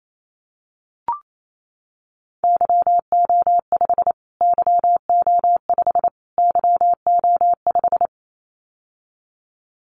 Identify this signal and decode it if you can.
Morse code, heard as a keyed tone, about 28 wpm, 700 Hz. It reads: YO5 YO5 YO5
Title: Callsign Prefixes for Top 25 DXCC - Repeated 3x - 28wpm